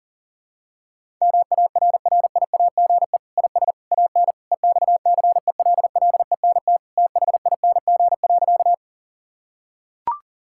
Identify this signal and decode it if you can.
Morse code, heard as a keyed tone, about 40 wpm, 700 Hz. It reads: MARRIAGE IS AN EXCELLENT THING.
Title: Sentences from Top 1000 Words - Review Part 5 - 40wpm